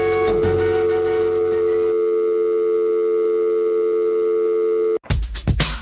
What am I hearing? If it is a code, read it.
Dial tone (North America), 350 + 440 Hz.